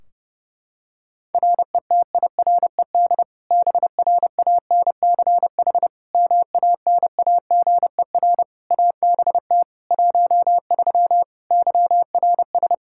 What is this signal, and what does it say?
Morse code, heard as a keyed tone, about 30 wpm, 700 Hz. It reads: RETIRED BRANCH MANAGER ABT 13 YRS
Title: K1USN SST QSO Practice 30wpm